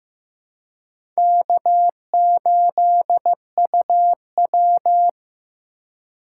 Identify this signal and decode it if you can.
Morse code, heard as a keyed tone, about 15 wpm, 700 Hz. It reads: K8UW
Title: USA Call Signs - 3x Character Spacing - 15wpm